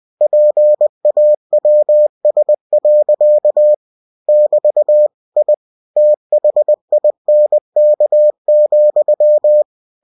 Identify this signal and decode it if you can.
Morse code, heard as a keyed tone, about 20 wpm, 600 Hz. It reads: PAWS. <BT> I THINK,
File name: cw-books / winnie-the-pooh / winnie-20wpm-page-0024